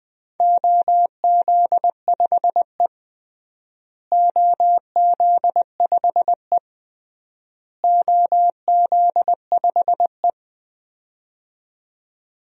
Morse code, heard as a keyed tone, about 21 wpm, 700 Hz. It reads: OZ5E OZ5E OZ5E
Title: DXCC-100 Call Signs - 4 Characters - Repeated 3x - 20wpm